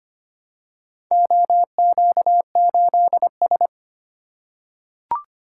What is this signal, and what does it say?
Morse code, heard as a keyed tone, about 25 wpm, 700 Hz. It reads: OQ8S
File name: practice / 25wpm / Four Letter-Number Combinations - 25wpm